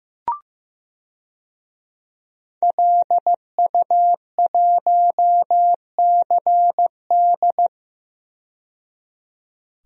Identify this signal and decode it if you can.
Morse code, heard as a keyed tone, about 15 wpm, 700 Hz. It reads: LU1CD
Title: DXCC-100 Call Signs - 5 Characters - 15wpm